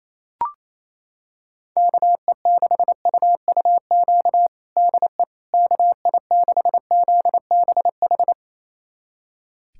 Morse code, heard as a keyed tone, about 28 wpm, 700 Hz. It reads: KE6UUQ DE KI6ZBH
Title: USA Callsign DE Callsign 28wpm